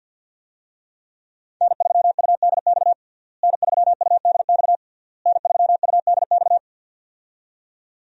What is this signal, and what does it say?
Morse code, heard as a keyed tone, about 51 wpm, 690 Hz. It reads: N3UDX N3UDX N3UDX
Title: USA Call Signs - 5 Characters - Repeated 3x - 50wpm